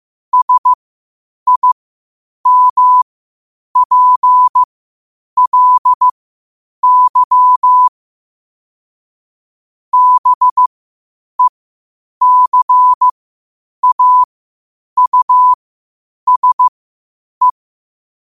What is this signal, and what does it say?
Morse code, heard as a keyed tone, about 15 wpm, spaced as for 8 wpm, 1000 Hz. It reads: SIMPLY BECAUSE
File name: qotdfiles / 8wpm